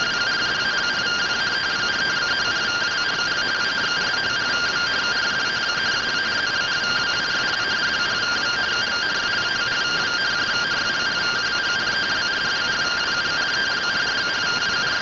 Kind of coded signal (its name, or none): two-tone teleprinter signal